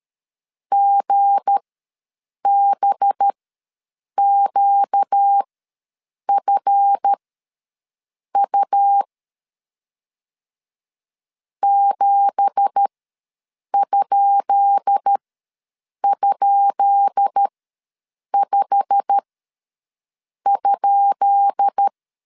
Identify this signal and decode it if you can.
Morse code, heard as a keyed tone, about 13 wpm, spaced as for 7 wpm, 800 Hz. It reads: GBQFU 7??5?
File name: LEZIONI / 10 SIGNS / 10B INT ws 12wpm 3527